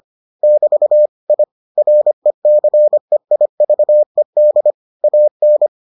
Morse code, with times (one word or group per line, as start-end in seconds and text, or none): 0.4-1.1 <BT>
1.3-1.4 I
1.8-4.7 RECEIVED
5.0-5.7 AN